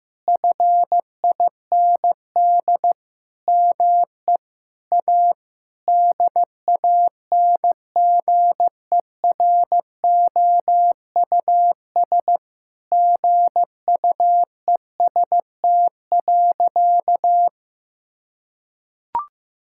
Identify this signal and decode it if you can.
Morse code, heard as a keyed tone, about 15 wpm, 700 Hz. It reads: FIND ME A DANGEROUS GUEST.
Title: Sentences from Top 1600 Words 15wpm